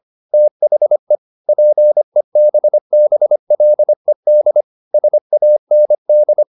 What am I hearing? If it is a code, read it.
Morse code, heard as a keyed tone, about 25 wpm, 600 Hz. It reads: THE PEBBLED SAND
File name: the old man and the sea part 1 25wpm